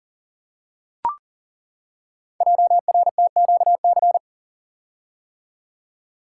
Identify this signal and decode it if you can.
Morse code, heard as a keyed tone, about 41 wpm, 700 Hz. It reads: JRTQC
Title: Code Groups - 5 Letters - 40wpm